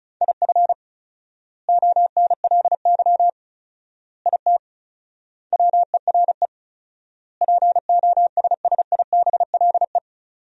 Morse code, heard as a keyed tone, about 36 wpm, 700 Hz. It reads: IF ONLY IT WERE POSSIBLE
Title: 4x Word-Spacing - Sentences from Top 500 Words - 35wpm